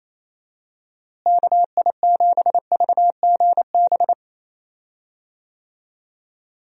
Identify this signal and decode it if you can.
Morse code, heard as a keyed tone, about 28 wpm, 700 Hz. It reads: KI7VGB
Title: USA Call Signs - 6 Characters - Repeated 2x - 28wpm